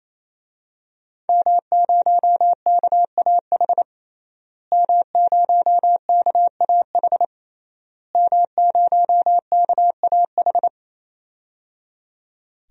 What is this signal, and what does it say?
Morse code, heard as a keyed tone, about 27 wpm, 700 Hz. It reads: M0KAH M0KAH M0KAH
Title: DXCC-100 Call Signs - 5 Characters - Repeated 3x - 28wpm